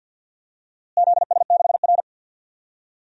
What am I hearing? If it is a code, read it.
Morse code, heard as a keyed tone, about 53 wpm, 700 Hz. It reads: ZS6R
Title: DXCC-100 Call Signs - 4 Characters - Repeated 2x - 50wpm